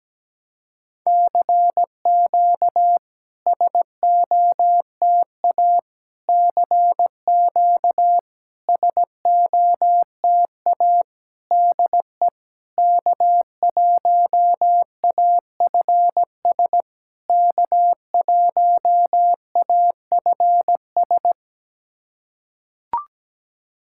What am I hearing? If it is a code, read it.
Morse code, heard as a keyed tone, about 17 wpm, 690 Hz. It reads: CQ SOTA CQ SOTA DE K1AFS K1AFS
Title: SOTA QSO Practice 17wpm